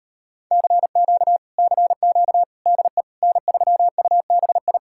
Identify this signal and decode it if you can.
Morse code, heard as a keyed tone, about 33 wpm, 700 Hz. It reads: CQ CQ DE N3UBI